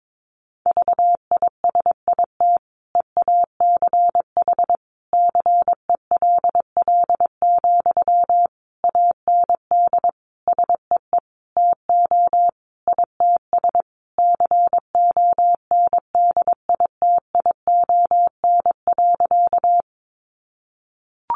Morse code, heard as a keyed tone, about 22 wpm, 700 Hz. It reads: VISIT EACH CELL, AND SEE TO ITS CONDITION.
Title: Sentences from Top 1100 Words 22wpm